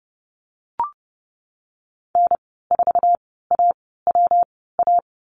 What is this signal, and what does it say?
Morse code, heard as a keyed tone, about 30 wpm, 700 Hz. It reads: N 4 A W A
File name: Lesson 19g - Letter w - Call Signs 30wpm